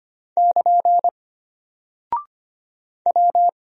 Morse code, heard as a keyed tone, about 25 wpm, 700 Hz. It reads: <KN> W
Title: Single Character (Letter:Number:Punctuation:Prosign) Alt 25wpm